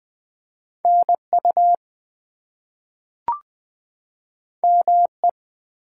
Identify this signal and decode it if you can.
Morse code, heard as a keyed tone, about 20 wpm, 700 Hz. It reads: NU ME
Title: US States and Canadian Provinces 20wpm